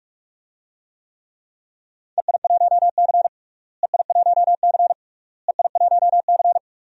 Morse code, heard as a keyed tone, about 44 wpm, 700 Hz. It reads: EI1C EI1C EI1C
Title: DXCC-100 Call Signs - 4 Characters - Repeated 3x - 45wpm